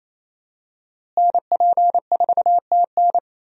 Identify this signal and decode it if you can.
Morse code, heard as a keyed tone, about 28 wpm, 700 Hz. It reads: NP4TN
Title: USA Call Signs - 5 Characters - Repeated 2x - 28wpm